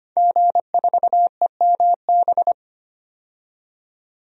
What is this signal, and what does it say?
Morse code, heard as a keyed tone, about 25 wpm, 700 Hz. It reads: G4EMB
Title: DXCC-100 Call Signs - 5 Characters - 25wpm